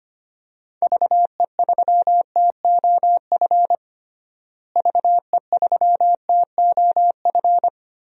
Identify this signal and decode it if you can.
Morse code, heard as a keyed tone, about 25 wpm, 700 Hz. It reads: VE3TOF VE3TOF
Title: DXCC-100 Call Signs - 6 Characters - Repeated 3x - 25wpm